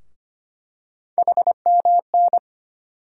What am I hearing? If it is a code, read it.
Morse code, heard as a keyed tone, about 25 wpm, 700 Hz. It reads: HMN
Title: USA Call Sign Postfixes - 3 Letters 25wpm